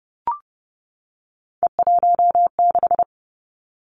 Morse code, heard as a keyed tone, about 30 wpm, 700 Hz. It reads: E16
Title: Three Challenging Letter-Number Combinations 30wpm